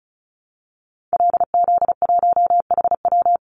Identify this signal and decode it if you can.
Morse code, heard as a keyed tone, about 34 wpm, 700 Hz. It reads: LZ1HW
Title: Call Signs Repeated 2x - 35wpm